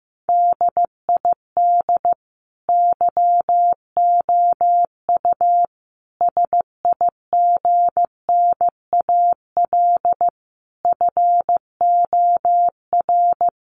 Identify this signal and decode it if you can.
Morse code, heard as a keyed tone, about 14 wpm, 700 Hz. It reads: DID YOU SIGNAL FOR
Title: Sentences from Top 1400 Words 15wpm